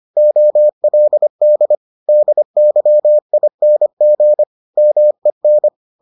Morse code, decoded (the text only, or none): OLD DYING MEN